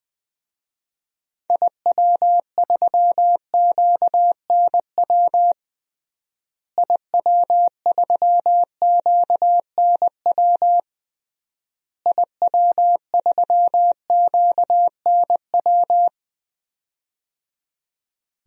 Morse code, heard as a keyed tone, about 20 wpm, 700 Hz. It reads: IW3QNW IW3QNW IW3QNW
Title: DXCC-100 Call Signs - 6 Characters - Repeated 3x - 20wpm